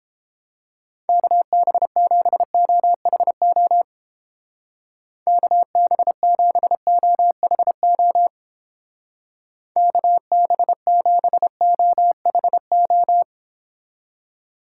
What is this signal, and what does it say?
Morse code, heard as a keyed tone, about 29 wpm, 700 Hz. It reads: KB7OHO KB7OHO KB7OHO